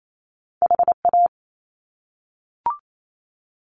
Morse code, heard as a keyed tone, about 28 wpm, 700 Hz. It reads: HA
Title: Top Two Letter Combinations in Words 28wpm